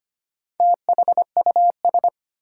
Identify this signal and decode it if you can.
Morse code, heard as a keyed tone, about 25 wpm, 700 Hz. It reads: THUS
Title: Sentences from Top 800 Words - New Word Drill - 25wpm